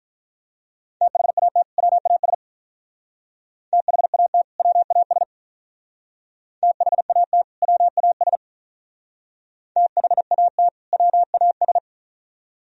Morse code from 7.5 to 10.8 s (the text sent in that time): WAS THAT